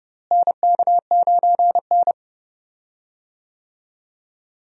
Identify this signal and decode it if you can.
Morse code, heard as a keyed tone, about 32 wpm, 700 Hz. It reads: NK9N
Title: USA Call Signs - 4 Characters - Repeated 3x - 30wpm